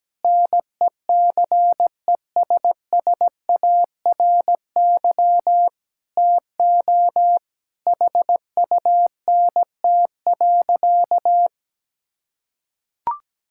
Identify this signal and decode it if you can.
Morse code, heard as a keyed tone, about 17 wpm, 700 Hz. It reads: NECESSARY TO HUNT.
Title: Sentences from Top 1600 Words 17wpm